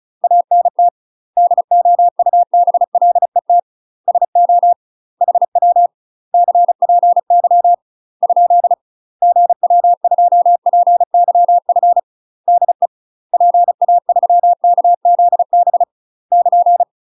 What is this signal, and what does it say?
Morse code, heard as a keyed tone, about 36 wpm, 700 Hz. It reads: ANT DOUBLET SO HW CPY ? GW2PYF DE PA3KZB <KN>
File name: UK MASTER QSOs TRAINING / 35WPM / UK MASTER QSOs 0022